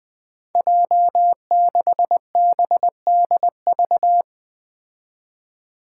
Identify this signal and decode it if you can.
Morse code, heard as a keyed tone, about 20 wpm, 700 Hz. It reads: J6BDV
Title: Five Challenging Letter-Number Combinations 20wpm